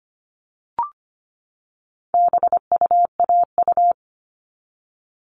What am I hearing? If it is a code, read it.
Morse code, heard as a keyed tone, about 25 wpm, 700 Hz. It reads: BUAU
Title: All Cut Numbers - 4-Digit 25wpm